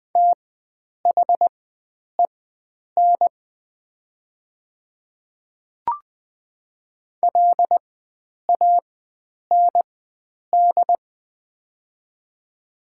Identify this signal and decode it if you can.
Morse code, heard as a keyed tone, about 20 wpm, spaced as for 9 wpm, 700 Hz. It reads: THEN LAND
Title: 4x Character Spacing - Top 300 Words - 20wpm